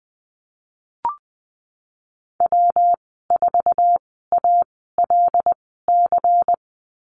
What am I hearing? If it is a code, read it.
Morse code, heard as a keyed tone, about 20 wpm, spaced as for 14 wpm, 700 Hz. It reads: W4ALC
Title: 179 Lesson 29g - Slash - Call Signs 20wpm